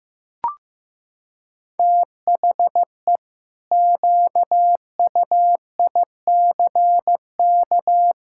Morse code, heard as a keyed tone, about 15 wpm, 700 Hz. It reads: THE QUICK
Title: Pangram Challenge v2 - 15wpm